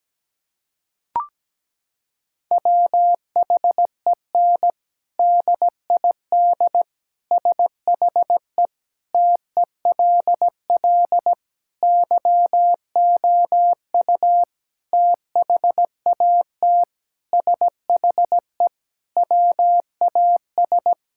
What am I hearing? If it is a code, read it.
Morse code, heard as a keyed tone, about 17 wpm, 700 Hz. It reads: WHEN DID SHE TELL YOU THAT SHE WAS
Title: Sentences from Top 1000 Words - Review Part 6 - 17wpm